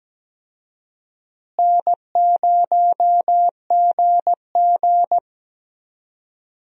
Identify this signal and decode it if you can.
Morse code, heard as a keyed tone, about 17 wpm, 700 Hz. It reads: N0GG
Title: USA Call Signs - 4 Characters - Repeated 3x - 17wpm